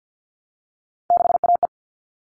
Morse code, heard as a keyed tone, about 50 wpm, 700 Hz. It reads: 6AE
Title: All Cut Numbers - 3-Digit 50wpm